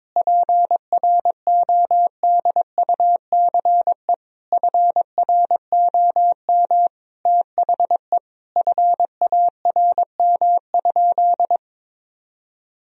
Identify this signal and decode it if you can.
Morse code, heard as a keyed tone, about 22 wpm, 690 Hz. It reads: PRODUCE FROM THE FARM?
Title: Sentences from Top 1000 Words - Review Part 3 - 22wpm